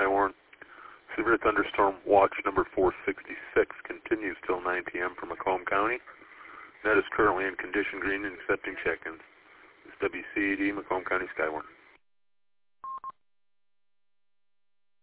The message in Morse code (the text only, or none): none